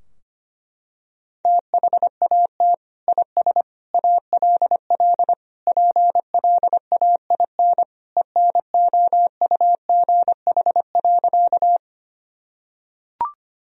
Morse code, heard as a keyed tone, about 25 wpm, 700 Hz. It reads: THAT IS ALL PLAIN ENOUGH.
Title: Sentences from Top 1000 Words - Review Part 2 - 25wpm